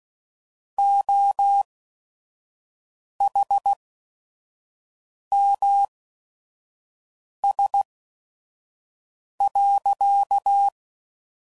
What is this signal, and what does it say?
Morse code, heard as a keyed tone, about 16 wpm, spaced as for 5 wpm, 800 Hz. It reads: OHMS.
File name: CW Code Groups @5WPM / Technical Subjects / XL